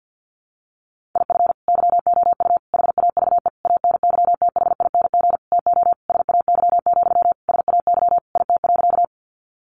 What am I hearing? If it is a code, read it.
Morse code, heard as a keyed tone, about 50 wpm, 700 Hz. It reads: IF YOU HAVE ANYTHING TO SAY, SAY IT.
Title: Sentences from Top 1000 Words 50wpm